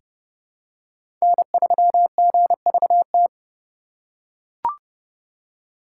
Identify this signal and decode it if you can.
Morse code, heard as a keyed tone, about 31 wpm, 700 Hz. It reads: N3GVT
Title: USA Call Signs - 5 Characters - 4x Character Spacing - 30wpm